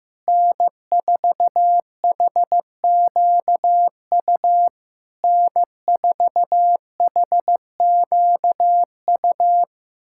Morse code, heard as a keyed tone, about 15 wpm, 700 Hz. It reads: N4HQU N4HQU
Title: SOTA QSO Practice 15wpm